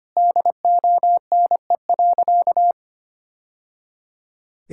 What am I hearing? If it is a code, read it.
Morse code, heard as a keyed tone, about 25 wpm, 700 Hz. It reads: DONE.
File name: practice / 25wpm / Sentences with 3-Word Phrases - 25wpm